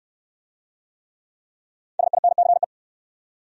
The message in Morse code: SINCE

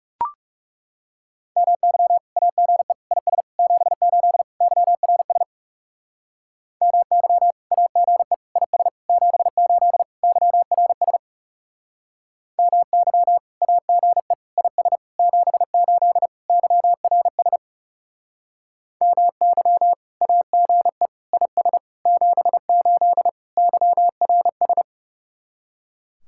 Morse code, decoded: MY AGE IS 78 YRS MY AGE IS 78 YRS MY AGE IS 78 YRS MY AGE IS 78 YRS

